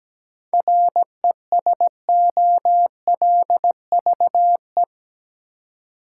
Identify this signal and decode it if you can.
Morse code, heard as a keyed tone, about 17 wpm, 700 Hz. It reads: RESOLVE
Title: Seven Letter Words - 17wpm